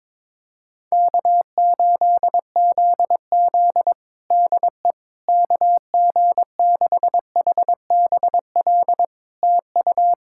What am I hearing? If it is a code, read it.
Morse code, heard as a keyed tone, about 22 wpm, 700 Hz. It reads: K8ZZ DE KG6HBL TU
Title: Practice QSO - Exchange 4 - Ending - 22wpm